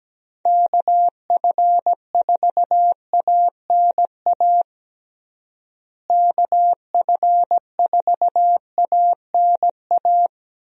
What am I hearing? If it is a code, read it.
Morse code, heard as a keyed tone, about 17 wpm, 700 Hz. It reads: KF4ANA KF4ANA
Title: USA Call Signs - 6 Characters - Repeated 3x - 17wpm